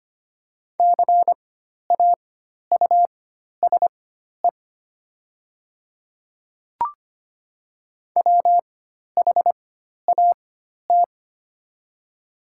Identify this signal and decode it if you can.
Morse code, heard as a keyed tone, about 25 wpm, spaced as for 12 wpm, 700 Hz. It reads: CAUSE WHAT
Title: 4x Character Spacing - Top 300 Words - 25wpm